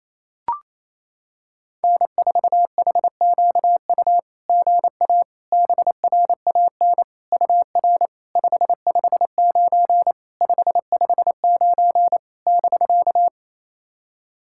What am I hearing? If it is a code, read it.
Morse code, heard as a keyed tone, about 28 wpm, 660 Hz. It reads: N4HQU GA BRAN UR 559 559 <BK>
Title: SOTA QSO Practice 28wpm